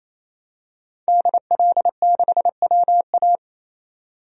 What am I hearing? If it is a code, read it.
Morse code, heard as a keyed tone, about 28 wpm, 700 Hz. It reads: DL6WA
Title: DXCC-100 Call Signs - 5 Characters - 28wpm